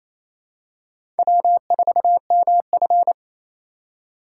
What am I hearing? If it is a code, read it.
Morse code, heard as a keyed tone, about 28 wpm, 700 Hz. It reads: W4MF
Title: USA Call Signs - 8x Character Spacing - 28wpm